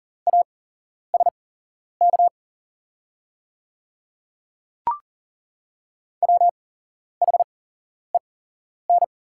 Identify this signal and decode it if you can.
Morse code, heard as a keyed tone, about 40 wpm, spaced as for 11 wpm, 690 Hz. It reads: ASK WHEN
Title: 8x Character Spacing - Top 200 Words - 40wpm